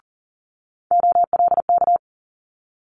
Morse code, heard as a keyed tone, about 41 wpm, 700 Hz. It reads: OLK
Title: Multi-letter Phonemes Quick Fire 40wpm